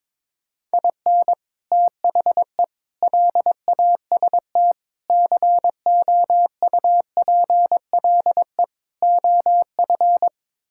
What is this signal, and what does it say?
Morse code, heard as a keyed tone, about 22 wpm, 700 Hz. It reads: IN THE LAST COUPLE OF
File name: practice / 22wpm / Build A Phrase - 5-Word Phrases - 22wpm